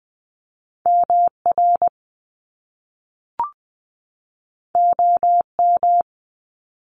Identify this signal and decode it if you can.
Morse code, heard as a keyed tone, about 20 wpm, 700 Hz. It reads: MR OM
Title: Top Two Letter Combinations in Words 20wpm